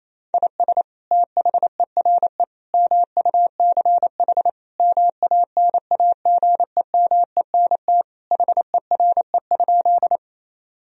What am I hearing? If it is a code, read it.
Morse code, heard as a keyed tone, about 27 wpm, 700 Hz. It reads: IS THERE MUCH MANAGEMENT HERE?
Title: Sentences from Top 1000 Words - Review Part 4 - 28wpm